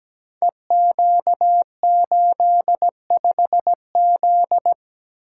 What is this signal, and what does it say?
Morse code, heard as a keyed tone, about 17 wpm, 700 Hz. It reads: EQ85Z
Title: Code Groups - 5 Letter:Numbers - 17wpm